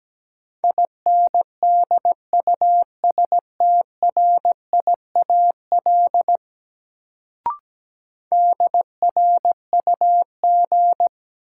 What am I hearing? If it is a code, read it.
Morse code, heard as a keyed tone, about 17 wpm, 700 Hz. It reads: INDUSTRIAL DRUG
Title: Sets of 5 Words - New Word Drill - 17wpm